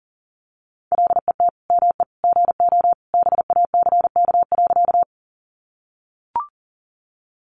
Morse code, heard as a keyed tone, about 40 wpm, 700 Hz. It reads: LET ME GO BACK.
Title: Sentences from Top 300 Words 40wpm